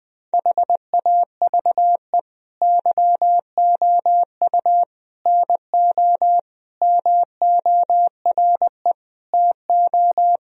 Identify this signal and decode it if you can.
Morse code, heard as a keyed tone, about 20 wpm, 700 Hz. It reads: HAVE YOU NO MORE TO
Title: Sentences from Top 1000 Words - Review Part 1 - 20wpm